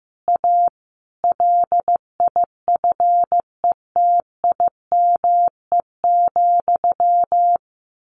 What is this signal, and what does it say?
Morse code, heard as a keyed tone, about 15 wpm, 700 Hz. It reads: A LIFETIME,
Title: A Sound Of Thunder 15wpm